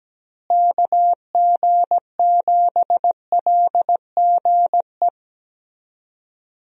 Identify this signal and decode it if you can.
Morse code, heard as a keyed tone, about 17 wpm, 700 Hz. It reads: KG7LGE
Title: USA Call Signs - 6 Characters - Repeated 3x - 17wpm